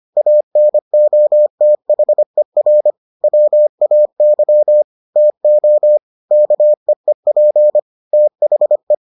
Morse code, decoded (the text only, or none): ANOTHER WAY TO KEEP THE